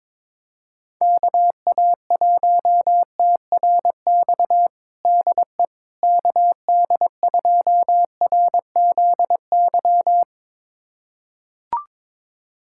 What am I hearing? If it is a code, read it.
Morse code, heard as a keyed tone, about 23 wpm, 700 Hz. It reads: KA1TRX DE KD2RZY
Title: USA Callsign DE Callsign - 6 Characters - 22wpm